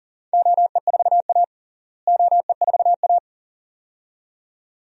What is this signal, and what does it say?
Morse code, heard as a keyed tone, about 40 wpm, 700 Hz. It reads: OE4A OE4A
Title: DXCC-100 Call Signs - 4 Characters - Repeated 2x - 40wpm